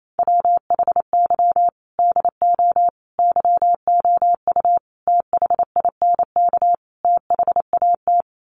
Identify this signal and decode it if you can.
Morse code, heard as a keyed tone, about 28 wpm, 700 Hz. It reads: WHY DO YOU THINK THAT
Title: Sets of 5 Words Encore2 28wpm